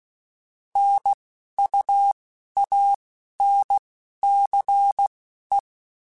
Morse code, heard as a keyed tone, about 16 wpm, spaced as for 11 wpm, 800 Hz. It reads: NUANCE